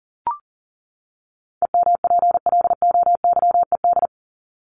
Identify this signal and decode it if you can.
Morse code, heard as a keyed tone, about 37 wpm, 700 Hz. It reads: EMPLOYED
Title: ED Suffix - Top 100 Words with Base Words 40wpm